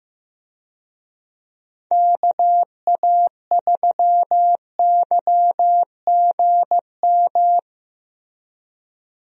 Morse code, heard as a keyed tone, about 15 wpm, 700 Hz. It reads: KA3YGM
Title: USA Call Signs - 6 Characters - Repeated 3x - 15wpm